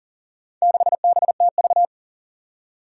Morse code, heard as a keyed tone, about 39 wpm, 700 Hz. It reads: 6BTV
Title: Four Challenging Letter-Number Combinations 40wpm